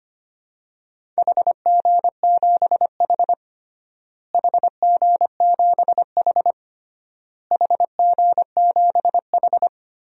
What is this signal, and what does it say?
Morse code, heard as a keyed tone, about 25 wpm, 700 Hz. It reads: HG7H HG7H HG7H
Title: DXCC-100 Call Signs - 4 Characters - Repeated 3x - 25wpm